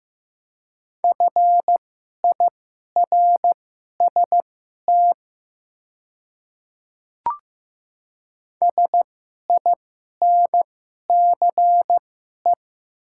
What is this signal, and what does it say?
Morse code, heard as a keyed tone, about 15 wpm, spaced as for 11 wpm, 700 Hz. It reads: FIRST SINCE
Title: 2x Character Spacing - Top 300 Words - 15wpm